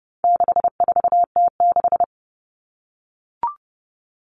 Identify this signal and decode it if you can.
Morse code, heard as a keyed tone, about 30 wpm, 700 Hz. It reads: 64T6
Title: Common Cut Numbers - 4-Digit 30wpm